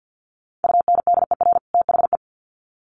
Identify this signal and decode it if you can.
Morse code, heard as a keyed tone, about 48 wpm, 700 Hz. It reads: UNDER THE
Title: Sets of 2 Words 50wpm